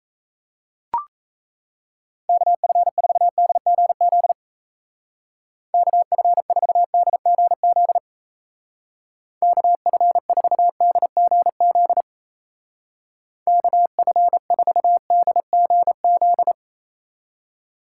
Morse code, heard as a keyed tone, about 37 wpm, 700 Hz. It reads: KF4DGZ KF4DGZ KF4DGZ KF4DGZ